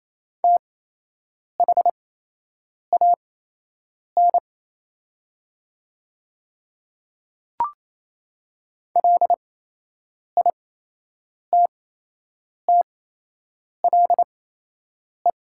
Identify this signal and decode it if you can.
Morse code, heard as a keyed tone, about 27 wpm, spaced as for 8 wpm, 690 Hz. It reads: THAN LITTLE
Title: 8x Character Spacing - Top 300 Words - 28wpm